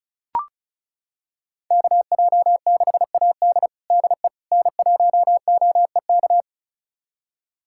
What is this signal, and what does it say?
Morse code, heard as a keyed tone, about 34 wpm, 700 Hz. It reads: KJ6AD DE N1OEK
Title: USA Callsign DE Callsign - 5 Characters - 35wpm